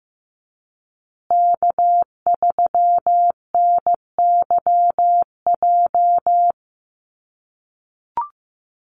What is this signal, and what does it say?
Morse code, heard as a keyed tone, about 15 wpm, 700 Hz. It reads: K3NYJ
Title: USA Call Signs Repeated 3x 15wpm